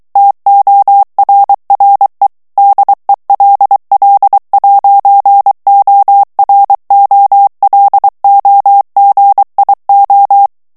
Morse code, heard as a keyed tone, about 23 wpm, 800 Hz. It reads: TORRE DELL'OROLOGIO
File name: ITALIEN texte clair 1 vitesse 22 mots minute